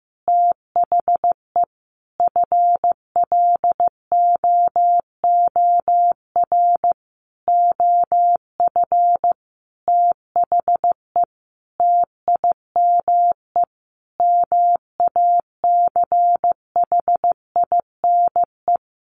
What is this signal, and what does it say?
Morse code, heard as a keyed tone, about 15 wpm, 700 Hz. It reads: THE FLOOR OF THE TIME MACHINE
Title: A Sound Of Thunder 15wpm